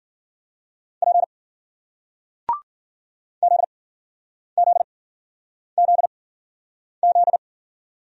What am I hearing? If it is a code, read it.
Morse code, heard as a keyed tone, about 54 wpm, spaced as for 9 wpm, 700 Hz. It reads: P ZZZZ